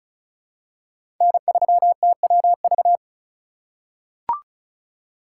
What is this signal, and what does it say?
Morse code, heard as a keyed tone, about 36 wpm, 700 Hz. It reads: N3TWV
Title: USA Call Signs - 5 Characters - 4x Character Spacing - 35wpm